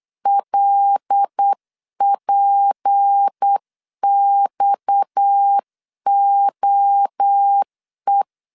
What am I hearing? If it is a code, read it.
Morse code, heard as a keyed tone, about 8 wpm, 800 Hz. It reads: LPXOE
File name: CW / DET / 1 6 1500C S C 8wpm 5219